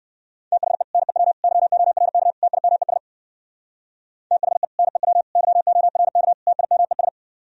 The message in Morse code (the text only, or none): THE NEW YORK TIMES THE NEW YORK TIMES